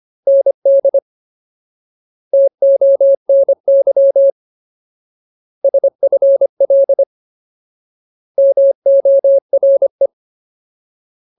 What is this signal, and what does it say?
Morse code, heard as a keyed tone, about 25 wpm, 550 Hz. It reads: ND TONY SFL MORE
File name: Ultimate Word List / 25WPM Ultimate